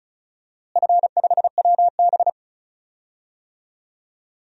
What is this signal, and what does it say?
Morse code, heard as a keyed tone, about 36 wpm, 700 Hz. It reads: F5WB
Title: DXCC-100 Call Signs - 4 Characters - Repeated 2x - 35wpm